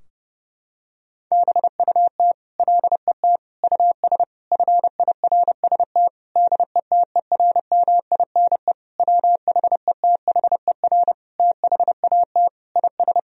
Morse code, heard as a keyed tone, about 30 wpm, 700 Hz. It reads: BUT LET US FIRST DETERMINE WHETHER THAT IS
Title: Sentences from Top 1000 Words - Review Part 4 - 30wpm